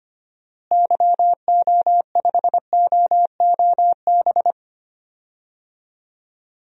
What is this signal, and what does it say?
Morse code, heard as a keyed tone, about 25 wpm, 700 Hz. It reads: YO5OOB
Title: DXCC-100 Call Signs - 6 Characters - Repeated 3x - 25wpm